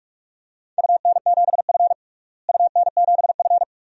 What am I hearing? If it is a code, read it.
Morse code, heard as a keyed tone, about 41 wpm, 700 Hz. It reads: UN7F UN7F
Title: DXCC-100 Call Signs - 4 Characters - Repeated 2x - 45wpm